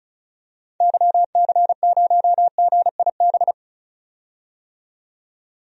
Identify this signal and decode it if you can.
Morse code, heard as a keyed tone, about 36 wpm, 700 Hz. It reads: YC0GIB
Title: DXCC-100 Call Signs - 6 Characters - 35wpm